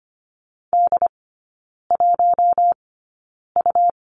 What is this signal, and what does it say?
Morse code, heard as a keyed tone, about 25 wpm, spaced as for 9 wpm, 700 Hz. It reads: D1U
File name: Lesson 17e - Letter u - Three Characters 25wpm